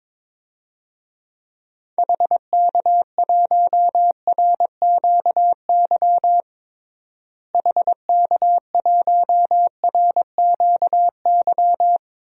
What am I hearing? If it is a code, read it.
Morse code, heard as a keyed tone, about 22 wpm, 700 Hz. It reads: HK1RQY HK1RQY
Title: DXCC-100 Call Signs - 6 Characters - Repeated 2x - 22wpm